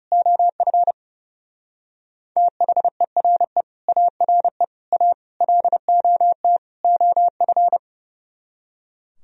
Morse code, heard as a keyed tone, about 36 wpm, 700 Hz. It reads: OF THERE ARE A LOT OF
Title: Speed-Racing - Sets of 5 Words 30wpm